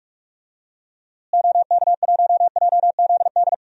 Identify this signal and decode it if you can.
Morse code, heard as a keyed tone, about 46 wpm, 700 Hz. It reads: OK1JZD
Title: DXCC-100 Call Signs - 6 Characters - Repeated 2x - 45wpm